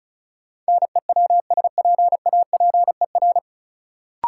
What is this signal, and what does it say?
Morse code, heard as a keyed tone, about 36 wpm, 700 Hz. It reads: NEWSPAPER
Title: Sentences from Top 1100 Words - New Word Drill Alt - 35wpm